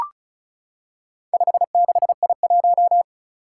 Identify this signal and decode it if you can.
Morse code, heard as a keyed tone, about 36 wpm, 700 Hz. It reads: 56I1
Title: Four Challenging Letter-Number Combinations 35wpm